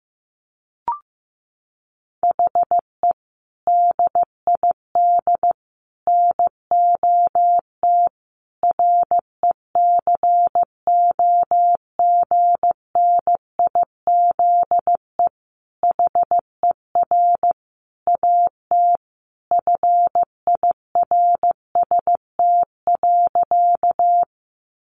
Morse code, heard as a keyed tone, about 14 wpm, 700 Hz. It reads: HE DID NOT RECOGNIZE HER AT FIRST.
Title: Sentences from Top 1400 Words 15wpm